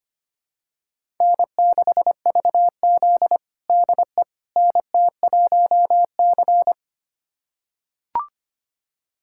Morse code, heard as a keyed tone, about 25 wpm, 700 Hz. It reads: N6VZ DE NT1C
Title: USA Callsign DE Callsign - 4 Characters - 25wpm